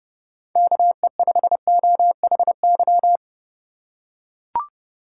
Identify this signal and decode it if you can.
Morse code, heard as a keyed tone, about 30 wpm, 700 Hz. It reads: KE5OHY
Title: USA Call Signs - 6 Characters - 3x Character Spacing - 30wpm